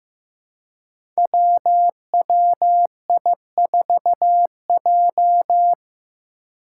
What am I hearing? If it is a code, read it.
Morse code, heard as a keyed tone, about 15 wpm, 700 Hz. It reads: WWI4J
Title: Five Challenging Letter-Number Combinations 15wpm